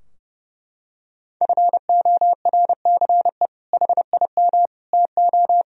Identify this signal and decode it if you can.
Morse code, heard as a keyed tone, about 30 wpm, 700 Hz. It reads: FORCE HIM TO
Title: Sentences from Top 1000 Words - Review Part 3 - 30wpm